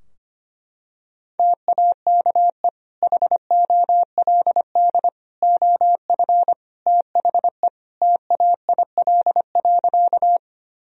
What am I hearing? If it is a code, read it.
Morse code, heard as a keyed tone, about 25 wpm, 700 Hz. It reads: TAKE HOLD OF THE TAIL.
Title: Sentences from Top 1000 Words - Review Part 3 - 25wpm